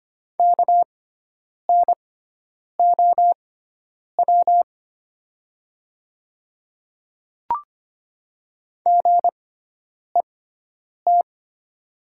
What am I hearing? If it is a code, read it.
Morse code, heard as a keyed tone, about 25 wpm, spaced as for 9 wpm, 700 Hz. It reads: KNOW GET
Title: 6x Character Spacing - Top 200 Words - 25wpm